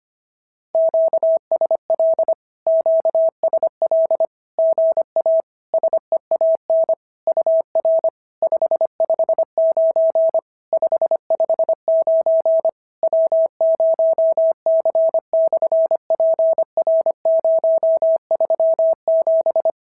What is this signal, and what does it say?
Morse code, heard as a keyed tone, about 24 wpm, 650 Hz. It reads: QSL QSL GA SEAN UR 559 559 W0C/PR037
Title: SOTA QSO Practice 25wpm